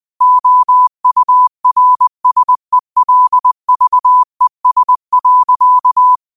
Morse code, heard as a keyed tone, about 20 wpm, 1000 Hz. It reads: OURSELVES.
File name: qotdfiles / 20wpm